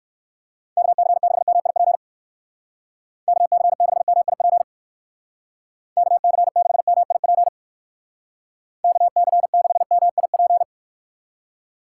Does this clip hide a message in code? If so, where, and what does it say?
Yes: Morse code from 0.8 to 10.6 s, heard as a keyed tone, about 59 wpm, 700 Hz. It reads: KC6MIP KC6MIP KC6MIP KC6MIP